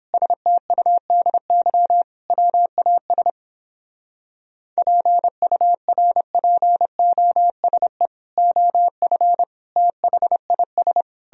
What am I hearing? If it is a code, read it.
Morse code, heard as a keyed tone, about 31 wpm, 700 Hz. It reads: STUDY WAS PURPOSE OF THIS